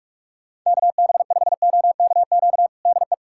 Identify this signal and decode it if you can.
Morse code, heard as a keyed tone, about 41 wpm, 700 Hz. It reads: KB5OKQ DE
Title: USA Callsign DE Callsign 45wpm